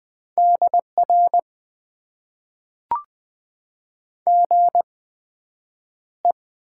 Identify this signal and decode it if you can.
Morse code, heard as a keyed tone, about 20 wpm, spaced as for 9 wpm, 700 Hz. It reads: DR GE
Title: Top 90 QSO Elements - 8x Character Spacing - 20wpm